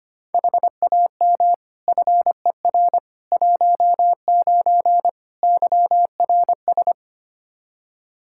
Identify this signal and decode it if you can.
Morse code, heard as a keyed tone, about 25 wpm, 700 Hz. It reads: HAM FER 19 YRS
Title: Practice QSO - Exchange 3 - More Info - 25wpm